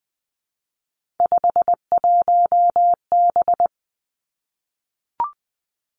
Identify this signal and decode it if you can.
Morse code, heard as a keyed tone, about 20 wpm, 700 Hz. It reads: 51B
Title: Three Challenging Letter-Number Combinations 20wpm